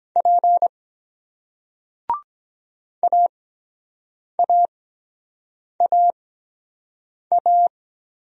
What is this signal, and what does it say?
Morse code, heard as a keyed tone, about 23 wpm, spaced as for 7 wpm, 700 Hz. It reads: P AAAA